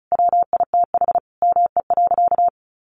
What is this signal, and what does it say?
Morse code, heard as a keyed tone, about 34 wpm, 700 Hz. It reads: WITH ME.
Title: Sentences from Top 700 Words 35wpm